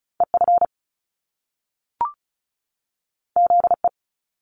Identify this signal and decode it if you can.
Morse code, heard as a keyed tone, about 36 wpm, 700 Hz. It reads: EF ZE
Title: Top Two Letter Combinations in Words 35wpm